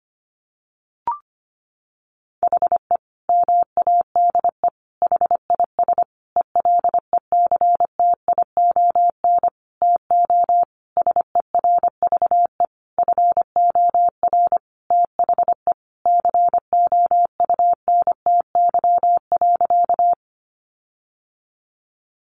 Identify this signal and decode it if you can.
Morse code, heard as a keyed tone, about 25 wpm, 700 Hz. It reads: HE MADE HIS ELECTION TO SERVE FOR THE COUNTY.
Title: Sentences from Top 1500 Words 25wpm